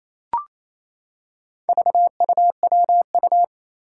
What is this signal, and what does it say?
Morse code, heard as a keyed tone, about 28 wpm, 700 Hz. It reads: VUWU
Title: Four Challenging Letter-Number Combinations 28wpm